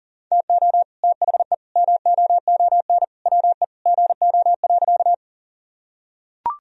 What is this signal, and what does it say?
Morse code, heard as a keyed tone, about 37 wpm, 700 Hz. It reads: TO THE MOON WE GO.
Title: Sentences from Top 1000 Words - Review Part 3 - 40wpm